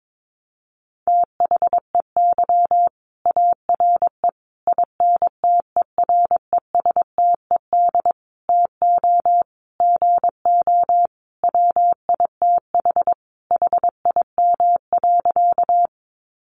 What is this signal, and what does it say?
Morse code, heard as a keyed tone, about 22 wpm, 700 Hz. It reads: THEY ARE INTERESTED TO GO WITH HIM.
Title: Sentences from Top 900 Words 22wpm